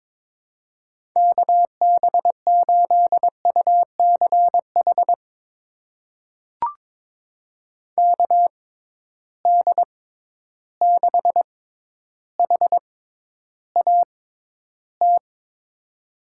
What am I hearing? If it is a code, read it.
Morse code, heard as a keyed tone, about 22 wpm, spaced as for 11 wpm, 700 Hz. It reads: KB8UCH KD6HAT
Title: USA Call Signs - 6 Characters - 6x Character Spacing - 22wpm